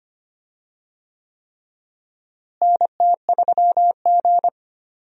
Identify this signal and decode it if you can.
Morse code, heard as a keyed tone, about 25 wpm, 700 Hz. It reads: NT3G
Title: USA Call Signs - 4 Characters - Repeated 3x - 25wpm